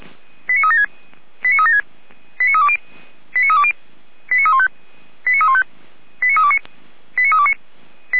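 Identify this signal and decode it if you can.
Five-tone selective call (ZVEI), heard as quick runs of five tones, tones of about 70 ms. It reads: 89376 89376 89320 89320 89315 89315 89329 89329
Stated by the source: ZVEI five-tone call 89230 received with an SDR by fire brigade Feuerwehr Orlamünde.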